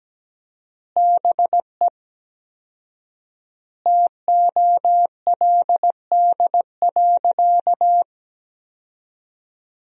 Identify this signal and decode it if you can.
Morse code, heard as a keyed tone, about 17 wpm, 700 Hz. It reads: BE TOLD.
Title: 4x Word-Spacing - Sentences from Top 400 Words - 17wpm